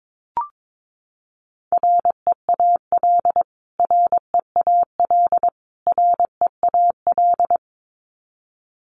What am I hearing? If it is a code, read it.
Morse code, heard as a keyed tone, about 22 wpm, 700 Hz. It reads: REAL REAL REAL
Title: Top 500 Words Repeated 22wpm